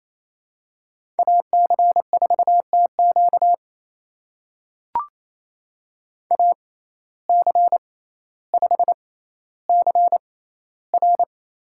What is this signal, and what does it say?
Morse code, heard as a keyed tone, about 28 wpm, spaced as for 14 wpm, 700 Hz. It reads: AC4TQ AC5CR
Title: USA Call Signs - 5 Characters - 6x Character Spacing - 28wpm